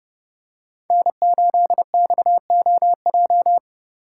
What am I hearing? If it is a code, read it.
Morse code, heard as a keyed tone, about 28 wpm, 700 Hz. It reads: N8XOJ